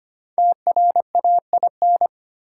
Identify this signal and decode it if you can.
Morse code, heard as a keyed tone, about 25 wpm, 700 Hz. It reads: TRAIN
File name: 5 Letter Words - First 3 Letters Given - 25wpm